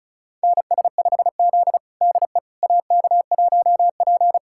Morse code, heard as a keyed tone, about 36 wpm, 700 Hz. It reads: NS5Z DE AK1P
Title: USA Callsign DE Callsign - 4 Characters - 35wpm